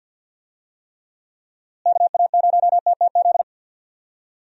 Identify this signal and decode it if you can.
Morse code, heard as a keyed tone, about 46 wpm, 700 Hz. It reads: KA0TTZ
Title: USA Call Signs - 6 Characters - Repeated 2x - 50wpm